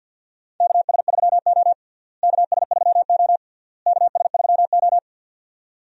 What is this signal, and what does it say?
Morse code, heard as a keyed tone, about 49 wpm, 700 Hz. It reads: KS3O KS3O KS3O
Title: USA Call Signs - 4 Characters - Repeated 3x - 50wpm